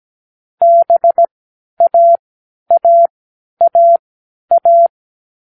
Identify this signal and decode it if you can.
Morse code, heard as a keyed tone, about 17 wpm, spaced as for 10 wpm, 680 Hz. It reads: BAAAA